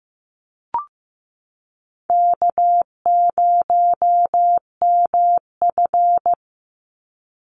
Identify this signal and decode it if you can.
Morse code, heard as a keyed tone, about 15 wpm, 700 Hz. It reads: K0MF
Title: USA Call Signs 15wpm